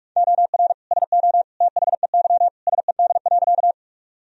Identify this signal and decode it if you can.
Morse code, heard as a keyed tone, about 46 wpm, 700 Hz. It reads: OR SO THEY SED.
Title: Sentences from Top 100 Words and Abbreviations 45wpm